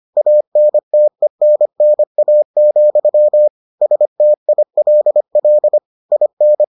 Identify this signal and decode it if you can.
Morse code, heard as a keyed tone, about 25 wpm, 600 Hz. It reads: ANTENNA, STILL IN